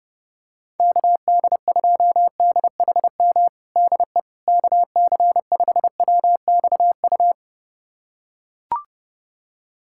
Morse code, heard as a keyed tone, about 30 wpm, 700 Hz. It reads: KD2DHM DE KC5WXU
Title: USA Callsign DE Callsign - 6 Characters - 30wpm